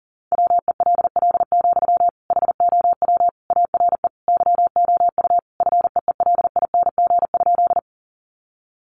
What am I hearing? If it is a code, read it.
Morse code, heard as a keyed tone, about 37 wpm, 700 Hz. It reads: WELL, HOW ARE YOU FEELING?
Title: Sentences from Top 1200 Words 40wpm